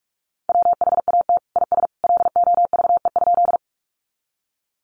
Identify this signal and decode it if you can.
Morse code, heard as a keyed tone, about 46 wpm, 700 Hz. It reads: WHAT IS LOVE?
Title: Sentences from Top 400 Words 45wpm